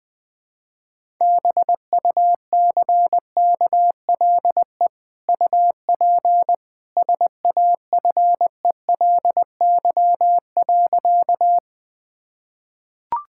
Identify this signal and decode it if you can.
Morse code, heard as a keyed tone, about 21 wpm, 700 Hz. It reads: BUCKLE UP SAFELY.
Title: Easy Sentences with Hints - 20wpm